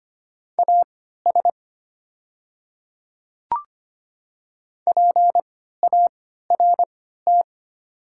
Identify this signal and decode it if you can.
Morse code, heard as a keyed tone, about 25 wpm, spaced as for 14 wpm, 700 Hz. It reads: AS PART
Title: 3x Character Spacing - Top 300 Words - 25wpm